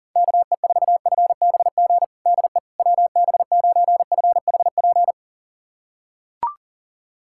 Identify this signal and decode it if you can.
Morse code, heard as a keyed tone, about 37 wpm, 700 Hz. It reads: KE4FBG DE WB9FHP
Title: USA Callsign DE Callsign - 6 Characters - 40wpm